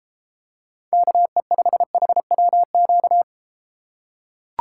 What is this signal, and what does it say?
Morse code, heard as a keyed tone, about 33 wpm, 700 Hz. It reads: KE5HWQ